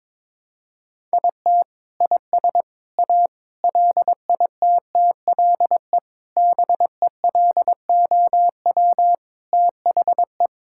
Morse code, heard as a keyed tone, about 22 wpm, 700 Hz. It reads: IT IS A LITTLE BELOW THE
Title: Sentences from Top 1000 Words - Review Part 5 - 22wpm